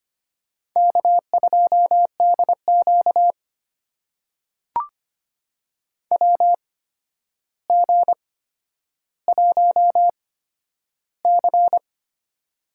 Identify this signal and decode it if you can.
Morse code, heard as a keyed tone, about 25 wpm, spaced as for 11 wpm, 700 Hz. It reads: K2DQ WG1C
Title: USA Call Signs - 8x Character Spacing - 25wpm